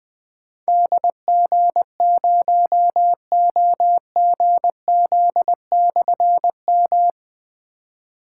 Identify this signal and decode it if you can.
Morse code, heard as a keyed tone, about 20 wpm, 700 Hz. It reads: DG0OGZ/M
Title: Difficult Call Signs 20wpm